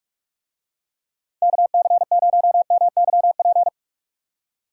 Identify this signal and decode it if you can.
Morse code, heard as a keyed tone, about 44 wpm, 700 Hz. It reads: KC0MYP